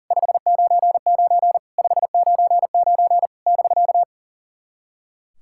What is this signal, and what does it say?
Morse code, heard as a keyed tone, about 42 wpm, 690 Hz. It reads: 599 599 <BK>
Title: SOTA QSO Practice 40wpm